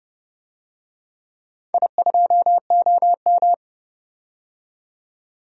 Morse code, heard as a keyed tone, about 32 wpm, 700 Hz. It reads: I2OM
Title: DXCC-100 Call Signs - 4 Characters - 30wpm